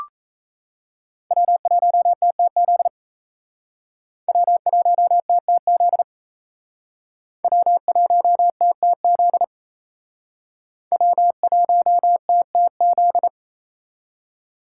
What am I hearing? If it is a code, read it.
Morse code, heard as a keyed tone, about 35 wpm, 700 Hz. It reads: W1TTZ W1TTZ W1TTZ W1TTZ